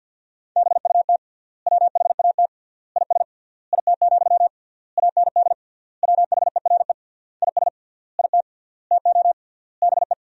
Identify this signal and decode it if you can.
Morse code, heard as a keyed tone, about 50 wpm, 700 Hz. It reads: BUT WHAT IS IT, AND WHERE IS IT TO BE
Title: 3x Word-Spacing - Sentences from Top 300 Words - 50wpm